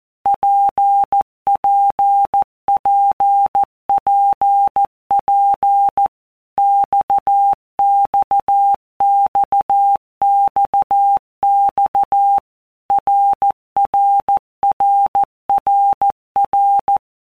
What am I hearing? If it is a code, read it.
Morse code, heard as a keyed tone, about 14 wpm, 800 Hz. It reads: PPPPP XXXXX RRRRR
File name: All Groups 1-minute Blocks / Groups@13 / RKPX